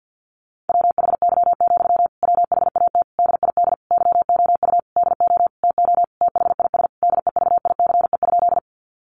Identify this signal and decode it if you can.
Morse code, heard as a keyed tone, about 50 wpm, 700 Hz. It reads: WHY, WHAT DID YOU DO TO THIS DEVICE?
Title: Sentences from Top 1500 Words 50wpm